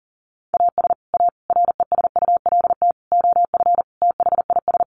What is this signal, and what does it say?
Morse code, heard as a keyed tone, about 40 wpm, 700 Hz. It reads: AS A RESULT OF THIS
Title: Sets of 5 Words 40wpm